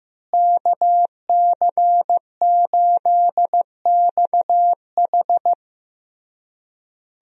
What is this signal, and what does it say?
Morse code, heard as a keyed tone, about 15 wpm, 700 Hz. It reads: KC8XH
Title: USA Call Signs - 5 Characters - Repeated 3x - 15wpm